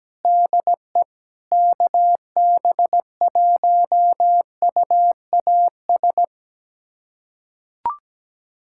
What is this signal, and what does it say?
Morse code, heard as a keyed tone, about 17 wpm, 700 Hz. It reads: DE KB1UAS
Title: USA Callsign DE Callsign - 6 Characters - 17wpm